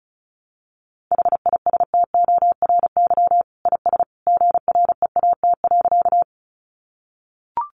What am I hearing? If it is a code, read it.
Morse code, heard as a keyed tone, about 34 wpm, 700 Hz. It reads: HISTORY IS GREAT.
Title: Sentences from Top 600 Words 35wpm